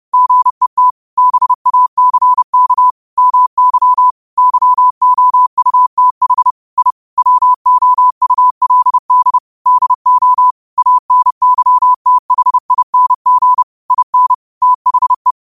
Morse code, heard as a keyed tone, about 30 wpm, 1000 Hz. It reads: GET BACK MY YOUTH I WOULD DO ANYTHING IN THE
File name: qotdfiles / 30wpm